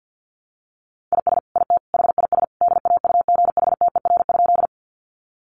Morse code, heard as a keyed tone, about 50 wpm, 700 Hz. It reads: IS IT HIS DAUGHTER?
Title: Sentences from Top 1500 Words 50wpm